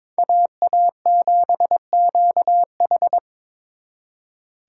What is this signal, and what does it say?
Morse code, heard as a keyed tone, about 25 wpm, 700 Hz. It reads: AA7QH
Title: Speed Racing - USA 2x2 Call Signs 22wpm